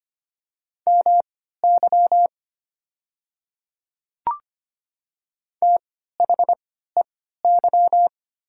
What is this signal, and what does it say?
Morse code, heard as a keyed tone, about 25 wpm, spaced as for 14 wpm, 700 Hz. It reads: MY THEY
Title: 3x Character Spacing - Top 100 Words - 25wpm